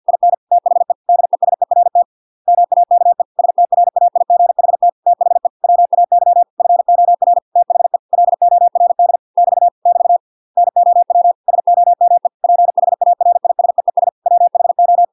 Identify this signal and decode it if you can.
Morse code, heard as a keyed tone, about 50 wpm, 700 Hz. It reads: IN THE DESERT MAKE STRAIGHT THE WAY FOR THE LORD <BT> <BT> NOW SOME PHARISEES WHO